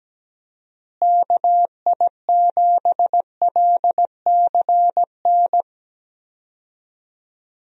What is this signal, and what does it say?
Morse code, heard as a keyed tone, about 17 wpm, 700 Hz. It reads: KI7LCN
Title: USA Call Signs - 6 Characters - Repeated 2x - 17wpm